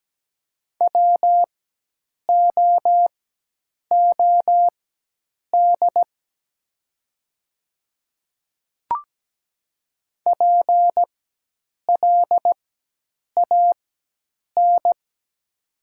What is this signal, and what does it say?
Morse code, heard as a keyed tone, about 17 wpm, spaced as for 8 wpm, 700 Hz. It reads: WOOD PLAN
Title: 4x Character Spacing - Top 500 Words - 17wpm